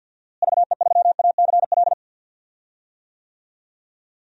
Morse code, heard as a keyed tone, about 53 wpm, 700 Hz. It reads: VE3ACL
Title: DXCC-100 Call Signs - 6 Characters - Repeated 3x - 50wpm